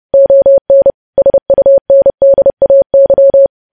Morse code, heard as a keyed tone, about 30 wpm, 550 Hz. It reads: ON SUNDAY
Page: news headlines in morse code at 30 wpm